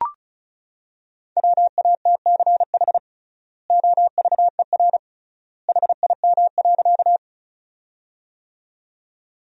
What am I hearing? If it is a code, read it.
Morse code, heard as a keyed tone, about 36 wpm, 700 Hz. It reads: WATCH OVER HIM.
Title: 3x Word-Spacing - Sentences from Top 300 Words - 35wpm